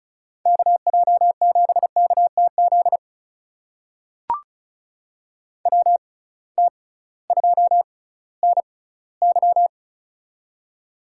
Morse code, heard as a keyed tone, about 36 wpm, spaced as for 19 wpm, 700 Hz. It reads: KJ7KTZ WT2NY
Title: USA Call Signs - 6x Character Spacing - 35wpm